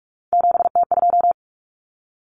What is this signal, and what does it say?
Morse code, heard as a keyed tone, about 44 wpm, 700 Hz. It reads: 7T2
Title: Common Cut Numbers - 3-Digit 45wpm